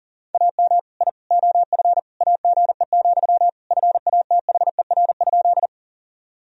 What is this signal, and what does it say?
Morse code, heard as a keyed tone, about 40 wpm, 700 Hz. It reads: AM I OF AGE, FATHER?
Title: Sentences from Top 1000 Words - Review Part 3 - 40wpm